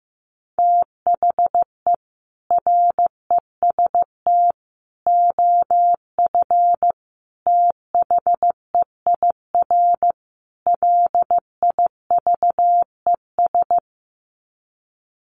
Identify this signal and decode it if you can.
Morse code, heard as a keyed tone, about 15 wpm, 700 Hz. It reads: THE REST OF THEIR LIVES
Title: Sets of 5 Words Encore 15wpm